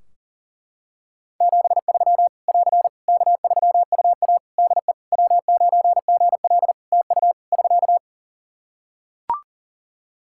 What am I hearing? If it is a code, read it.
Morse code, heard as a keyed tone, about 37 wpm, 700 Hz. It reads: 73 <AR> K3UA DE W9GL TU <SK>
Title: Practice QSO - Exchange 4 - Ending - 40wpm